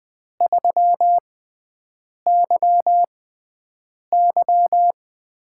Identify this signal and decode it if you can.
Morse code, heard as a keyed tone, about 20 wpm, spaced as for 7 wpm, 700 Hz. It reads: 3YY
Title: USA Call Signs - 6x Character Spacing - 20wpm